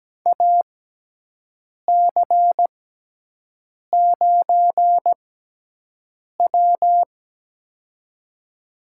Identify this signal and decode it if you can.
Morse code, heard as a keyed tone, about 17 wpm, spaced as for 6 wpm, 700 Hz. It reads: AC9W
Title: USA Call Signs - 6x Character Spacing - 17wpm